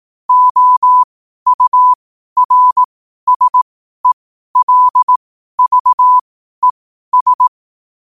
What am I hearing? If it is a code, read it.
Morse code, heard as a keyed tone, about 18 wpm, spaced as for 12 wpm, 1000 Hz. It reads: OURSELVES
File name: qotdfiles / 12wpm